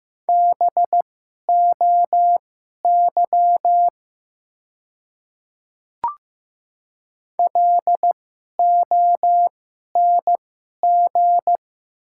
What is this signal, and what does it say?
Morse code, heard as a keyed tone, about 15 wpm, spaced as for 11 wpm, 700 Hz. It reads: BOY LONG
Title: 2x Character Spacing - Top 300 Words - 15wpm